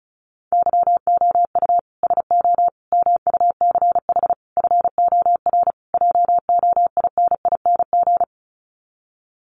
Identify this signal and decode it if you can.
Morse code, heard as a keyed tone, about 34 wpm, 700 Hz. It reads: YOU SO MUCH FOR JOINING
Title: Sets of 5 Words Encore2 35wpm